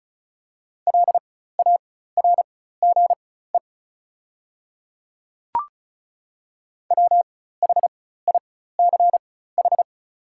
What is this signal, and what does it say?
Morse code, heard as a keyed tone, about 34 wpm, spaced as for 16 wpm, 700 Hz. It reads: LARGE WHICH
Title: 4x Character Spacing - Top 200 Words - 35wpm